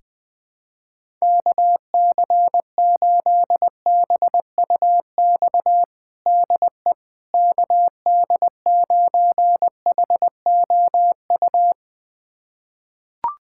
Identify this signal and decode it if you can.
Morse code, heard as a keyed tone, about 20 wpm, 700 Hz. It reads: KC8BUX DE KD9HOU
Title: USA Callsign DE Callsign - 6 Characters - 20wpm